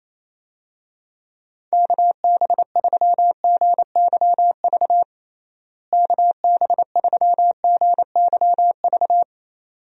Morse code, heard as a keyed tone, about 28 wpm, 700 Hz. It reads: KB3GYV KB3GYV
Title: USA Call Signs - 6 Characters - Repeated 2x - 28wpm